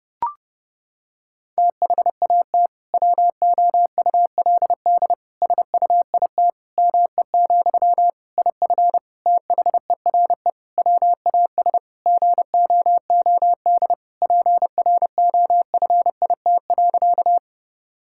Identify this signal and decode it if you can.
Morse code, heard as a keyed tone, about 32 wpm, 700 Hz. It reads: THAT WOULD SUIT ME, IF THERE WAS GOOD PROFIT.
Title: Sentences from Top 1600 Words 30wpm